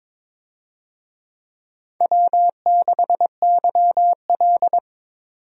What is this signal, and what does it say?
Morse code, heard as a keyed tone, about 22 wpm, 700 Hz. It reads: W6YL
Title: USA Call Signs - 4 Characters - 22wpm